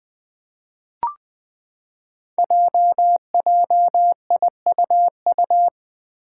Morse code, heard as a keyed tone, about 20 wpm, 700 Hz. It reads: JJIUU
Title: Five Challenging Letter-Number Combinations 20wpm